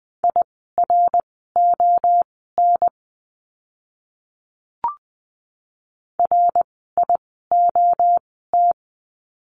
Morse code, heard as a keyed tone, about 20 wpm, spaced as for 15 wpm, 700 Hz. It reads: IRON RIOT
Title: 061 Lesson 11f - Letter h - Words 20wpm